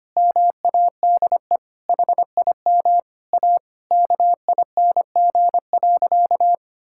Morse code, heard as a keyed tone, about 25 wpm, 700 Hz. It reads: MADE HIM A KING.
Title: Sentences from Top 1000 Words - Review Part 5 - 25wpm